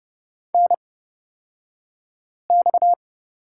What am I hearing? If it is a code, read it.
Morse code, heard as a keyed tone, about 30 wpm, spaced as for 5 wpm, 700 Hz. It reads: NX
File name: Lesson 37d - Letter x - Two Characters 30wpm